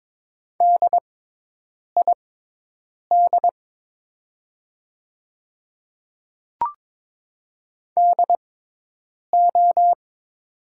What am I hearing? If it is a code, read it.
Morse code, heard as a keyed tone, about 22 wpm, spaced as for 8 wpm, 700 Hz. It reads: DID DO
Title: 6x Character Spacing - Top 100 Words - 22wpm